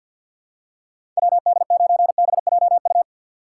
Morse code, heard as a keyed tone, about 51 wpm, 700 Hz. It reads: WD9BJU
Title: USA Call Signs - 6 Characters - 4x Character Spacing - 50wpm